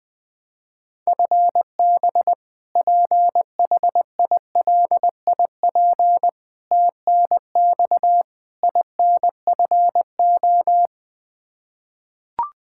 Morse code, heard as a keyed tone, about 20 wpm, 700 Hz. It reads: FB PHILIP TNX INFO
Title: Practice QSO - Exchange 3 - More Info - 20wpm